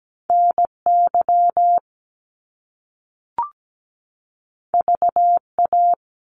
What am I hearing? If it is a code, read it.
Morse code, heard as a keyed tone, about 17 wpm, 700 Hz. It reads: NY VA
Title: US States 17wpm